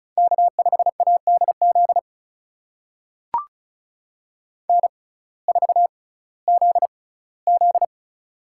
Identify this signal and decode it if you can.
Morse code, heard as a keyed tone, about 36 wpm, spaced as for 19 wpm, 700 Hz. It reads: K5ADZ N4ZZ
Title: USA Call Signs - 6x Character Spacing - 35wpm